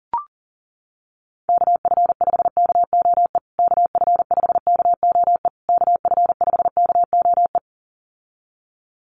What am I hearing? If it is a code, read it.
Morse code, heard as a keyed tone, about 40 wpm, 700 Hz. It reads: KF5KOE KF5KOE KF5KOE
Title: USA Call Signs Repeated 3x 40wpm